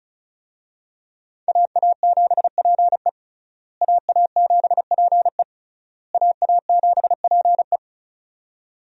Morse code, heard as a keyed tone, about 34 wpm, 700 Hz. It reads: AA7PE AA7PE AA7PE
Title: USA Call Signs - 5 Characters - Repeated 3x - 35wpm